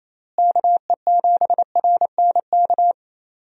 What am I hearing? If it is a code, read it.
Morse code, heard as a keyed tone, about 28 wpm, 700 Hz. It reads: KE7RNK
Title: USA Call Signs - 6 Characters - Repeated 2x - 28wpm